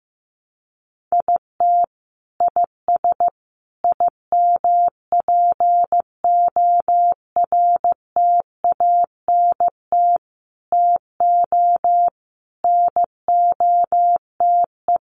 Morse code, heard as a keyed tone, about 15 wpm, 700 Hz. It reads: IT IS IMPORTANT TO NOTE
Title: Sets of 5 Words 15wpm